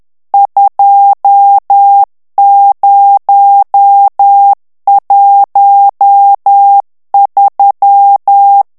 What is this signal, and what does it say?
Morse code, heard as a keyed tone, about 10 wpm, 800 Hz. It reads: 2013
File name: ANGLAIS texte clair 3 vitesse 10 mots minute